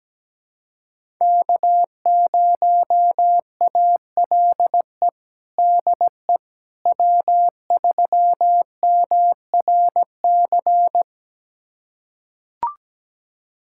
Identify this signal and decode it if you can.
Morse code, heard as a keyed tone, about 17 wpm, 700 Hz. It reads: K0ALE DE W3MRC
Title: USA Callsign DE Callsign - 5 Characters - 17wpm